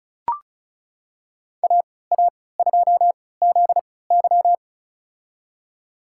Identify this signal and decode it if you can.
Morse code, heard as a keyed tone, about 36 wpm, spaced as for 20 wpm, 700 Hz. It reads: AA2ZY
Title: USA Call Signs - 5 Characters - 3x Character Spacing - 35wpm